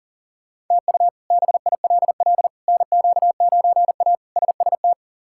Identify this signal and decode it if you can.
Morse code, heard as a keyed tone, about 40 wpm, 700 Hz. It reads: TU BILL NQ9A SST
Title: K1USN SST QSO Practice - 40wpm